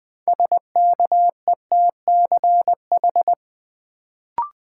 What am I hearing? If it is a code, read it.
Morse code, heard as a keyed tone, about 20 wpm, 700 Hz. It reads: SKETCH
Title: Build A Word - 2 Letters At A Time - 6-Letter Words - 20wpm